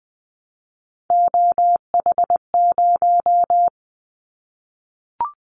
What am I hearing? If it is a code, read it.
Morse code, heard as a keyed tone, about 20 wpm, 700 Hz. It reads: OH0
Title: Callsign Prefixes for Top 100 DXCC - 20wpm